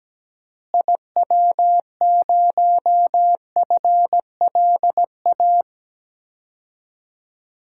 Morse code, heard as a keyed tone, about 17 wpm, 700 Hz. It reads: IW0FLA
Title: DXCC-100 Call Signs - 6 Characters - Repeated 3x - 17wpm